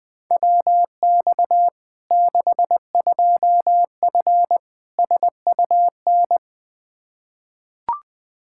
Morse code, heard as a keyed tone, about 21 wpm, 700 Hz. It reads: WX 62F SUN
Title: Practice QSO - Exchange 2 - Info - 20wpm